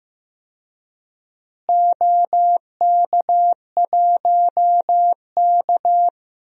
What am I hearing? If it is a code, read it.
Morse code, heard as a keyed tone, about 15 wpm, 700 Hz. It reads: OK1K
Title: DXCC-100 Call Signs - 4 Characters - Repeated 3x - 15wpm